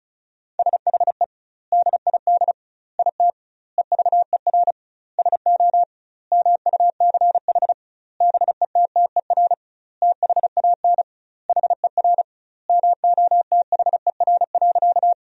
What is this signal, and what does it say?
Morse code, heard as a keyed tone, about 34 wpm, 700 Hz. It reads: SHE DID IT EVER SO MUCH BETTER THAN HER MOTHER.
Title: 2x Word-Spacing - Sentences from Top 400 Words - 35wpm